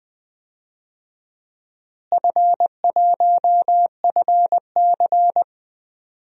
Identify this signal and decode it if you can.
Morse code, heard as a keyed tone, about 20 wpm, 700 Hz. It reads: F1FC
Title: DXCC-100 Call Signs - 4 Characters - Repeated 2x - 20wpm